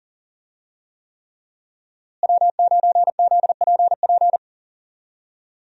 Morse code, heard as a keyed tone, about 40 wpm, 700 Hz. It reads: W9ZPP
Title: USA Call Signs - 5 Characters - Repeated 3x - 40wpm